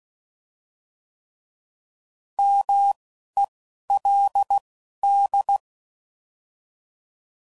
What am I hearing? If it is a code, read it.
Morse code, heard as a keyed tone, about 16 wpm, spaced as for 11 wpm, 800 Hz. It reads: MELD